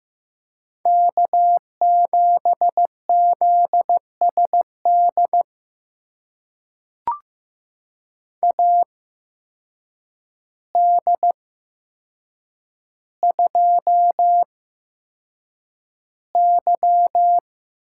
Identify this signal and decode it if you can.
Morse code, heard as a keyed tone, about 15 wpm, spaced as for 7 wpm, 700 Hz. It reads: K7ZSD AD2Y
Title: USA Call Signs - 8x Character Spacing - 15wpm